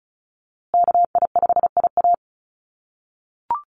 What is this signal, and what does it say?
Morse code, heard as a keyed tone, about 34 wpm, 700 Hz. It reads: KI5IA
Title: USA Call Signs Repeated 3x 35wpm